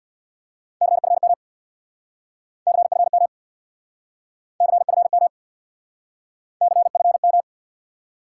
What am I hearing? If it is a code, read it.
Morse code, heard as a keyed tone, about 61 wpm, 700 Hz. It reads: CFM CFM CFM CFM
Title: Speed-Racing Top 180 QSO Elements 50wpm